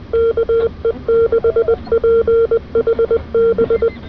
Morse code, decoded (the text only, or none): KE6PHB